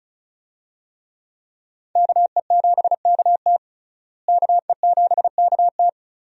Morse code, heard as a keyed tone, about 34 wpm, 700 Hz. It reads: KE7KT KE7KT
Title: USA Call Signs - 5 Characters - Repeated 3x - 35wpm